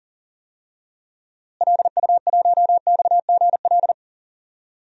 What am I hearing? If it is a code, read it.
Morse code, heard as a keyed tone, about 40 wpm, 700 Hz. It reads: LU1XGL
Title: DXCC-100 Call Signs - 6 Characters - Repeated 3x - 40wpm